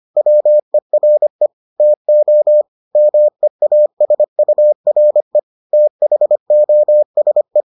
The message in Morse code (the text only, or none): WERE TO MEASURE THOSE